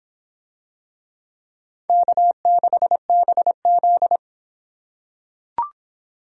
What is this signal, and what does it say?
Morse code, heard as a keyed tone, about 25 wpm, 700 Hz. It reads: K6BZ